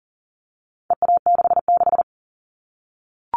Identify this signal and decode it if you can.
Morse code, heard as a keyed tone, about 40 wpm, 700 Hz. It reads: EA66
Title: All Cut Numbers - 4-Digit 40wpm